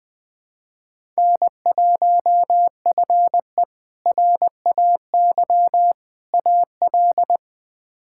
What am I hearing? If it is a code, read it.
Morse code, heard as a keyed tone, about 21 wpm, 700 Hz. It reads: N1FE RAY AL
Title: K1USN SST QSO Practice - 20wpm